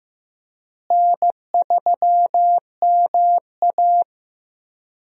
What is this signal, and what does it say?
Morse code, heard as a keyed tone, about 15 wpm, 700 Hz. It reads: N3MA
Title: USA Call Signs - 4 Characters - Repeated 3x - 15wpm